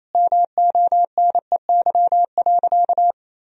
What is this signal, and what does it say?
Morse code, heard as a keyed tone, about 28 wpm, 700 Hz. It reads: MONEY.
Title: Sentences from Top 1000 Words - Review Part 2 - 28wpm